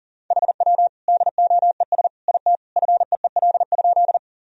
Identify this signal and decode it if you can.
Morse code, heard as a keyed tone, about 40 wpm, 700 Hz. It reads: HW DOES IT FEEL?
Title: Sentences from Top 400 Words and Abbreviations 40wpm